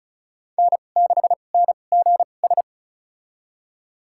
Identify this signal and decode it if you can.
Morse code, heard as a keyed tone, about 34 wpm, spaced as for 25 wpm, 700 Hz. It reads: N6NGS
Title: USA Call Signs - 5 Characters - 2x Character Spacing - 35wpm